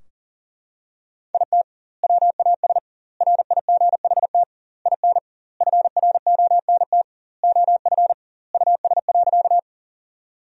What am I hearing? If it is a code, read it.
Morse code, heard as a keyed tone, about 40 wpm, 700 Hz. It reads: IT WAS RIGHT IN FRONT OF US.
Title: 2x Word-Spacing - Sentences from Top 500 Words - 40wpm